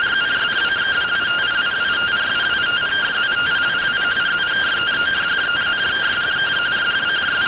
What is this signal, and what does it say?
Two-tone teleprinter signal, about 170 Hz shift.